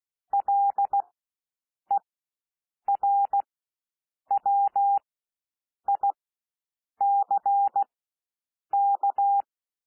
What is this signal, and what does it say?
Morse code, heard as a keyed tone, about 16 wpm, spaced as for 7 wpm, 800 Hz. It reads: LERWICK
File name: cwmp3 / W1 SOUND WORDS 16 wpm